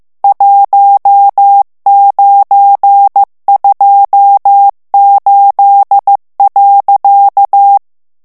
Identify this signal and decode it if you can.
Morse code, heard as a keyed tone, about 15 wpm, 800 Hz. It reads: 1928.
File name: morse mp3 / ITALIEN texte clair 1 vitesse 14 mots minute